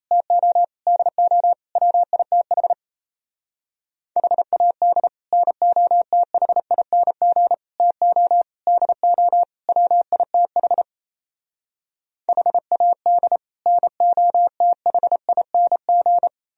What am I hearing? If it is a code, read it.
Morse code, heard as a keyed tone, about 37 wpm, 700 Hz. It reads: TO DO WITH HAD NOTHING TO DO WITH HAD NOTHING